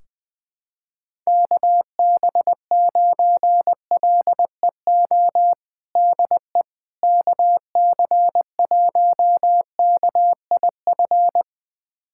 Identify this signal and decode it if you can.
Morse code, heard as a keyed tone, about 20 wpm, 700 Hz. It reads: KB9LEO DE KC1KIF
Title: USA Callsign DE Callsign - 6 Characters - 20wpm